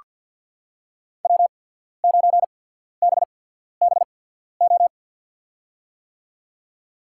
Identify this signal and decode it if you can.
Morse code, heard as a keyed tone, about 48 wpm, spaced as for 14 wpm, 700 Hz. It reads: W9BBO
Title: USA Call Signs - 5 Characters - 8x Character Spacing - 50wpm